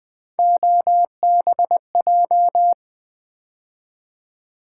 Morse code, heard as a keyed tone, about 21 wpm, 700 Hz. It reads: OBJ
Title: Build A Word - 3 Letters At A Time - 6-Letter Words - 20wpm